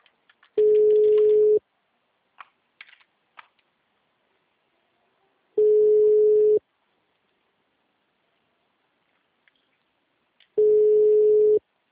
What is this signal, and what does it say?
Ringback tone (Europe, 425 Hz).
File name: out- -2004- - - .8160